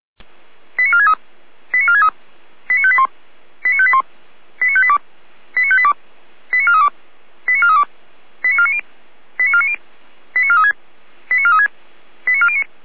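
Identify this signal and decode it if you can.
Five-tone selective call (ZVEI), heard as quick runs of five tones, tones of about 80 ms. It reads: 89463 89463 89581 89581 89582 89582 89432 89432 89490 89490 89436 89436 89409
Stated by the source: ZVEI five-tone call 89370 received by SDR, channel 455.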